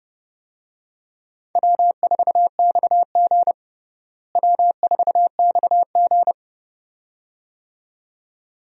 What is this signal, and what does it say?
Morse code, heard as a keyed tone, about 32 wpm, 700 Hz. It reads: W4XG W4XG
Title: USA Call Signs - 4 Characters - Repeated 2x - 30wpm